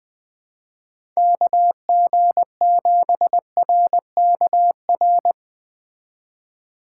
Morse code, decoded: KG7RKR

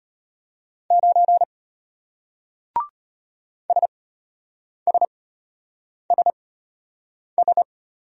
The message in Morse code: 9 SSSS

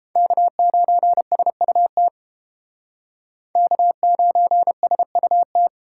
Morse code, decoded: K9SUT K9SUT

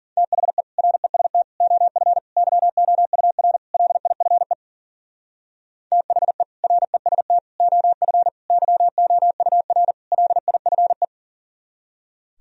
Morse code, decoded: THE REST OF YOUR LIFE THE REST OF YOUR LIFE

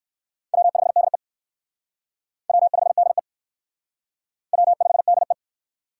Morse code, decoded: W5DE W5DE W5DE

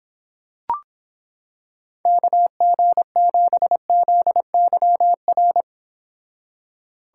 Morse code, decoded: KG7ZYR

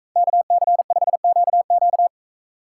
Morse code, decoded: KC5QQ

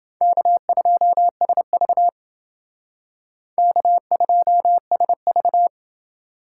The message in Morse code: K2SV K2SV

